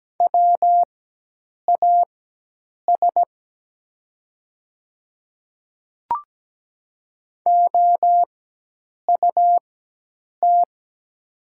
Morse code, heard as a keyed tone, about 17 wpm, spaced as for 8 wpm, 700 Hz. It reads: WAS OUT
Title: 4x Character Spacing - Top 100 Words - 17wpm